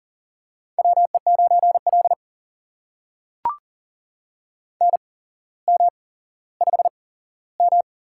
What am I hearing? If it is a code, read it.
Morse code, heard as a keyed tone, about 40 wpm, spaced as for 17 wpm, 700 Hz. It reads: WE9L NM5M
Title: USA Call Signs - 4 Characters - 8x Character Spacing - 40wpm